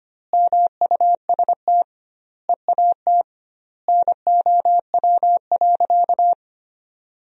Morse code, heard as a keyed tone, about 25 wpm, 700 Hz. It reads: MUST EAT NOW.
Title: 2x Word-Spacing - Sentences from Top 300 Words - 25wpm